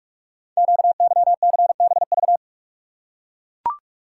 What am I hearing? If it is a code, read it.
Morse code, heard as a keyed tone, about 44 wpm, 690 Hz. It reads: QYCBV
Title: Code Groups - 5 Letters - 45wpm